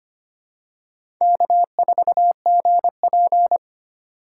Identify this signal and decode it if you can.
Morse code, heard as a keyed tone, about 25 wpm, 700 Hz. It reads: K4GP
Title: USA Call Signs - 4 Characters - 3x Character Spacing - 25wpm